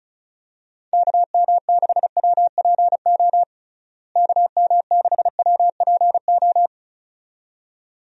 Morse code, heard as a keyed tone, about 36 wpm, 700 Hz. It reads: KM6WPO KM6WPO
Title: USA Call Signs - 6 Characters - Repeated 2x - 35wpm